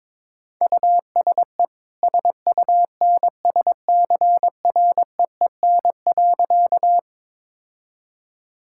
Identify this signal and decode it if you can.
Morse code, heard as a keyed tone, about 22 wpm, 690 Hz. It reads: USE SUNSCREEN.
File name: practice / 22wpm / Easy Sentences with Hints - 22wpm